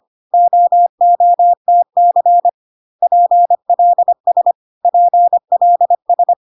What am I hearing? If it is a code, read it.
Morse code, heard as a keyed tone, about 25 wpm, 700 Hz. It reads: OOTC PLS PLS
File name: CW abbreviations alphabetical repeated 3 times 25 25wpm